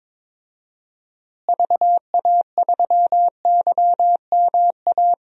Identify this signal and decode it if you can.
Morse code, heard as a keyed tone, about 22 wpm, 700 Hz. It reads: VA3YMA
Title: DXCC-100 Call Signs - 6 Characters - Repeated 2x - 22wpm